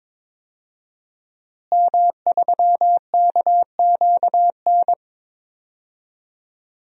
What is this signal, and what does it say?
Morse code, heard as a keyed tone, about 21 wpm, 700 Hz. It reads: M3KQN
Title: DXCC-100 Call Signs - 5 Characters - Repeated 3x - 22wpm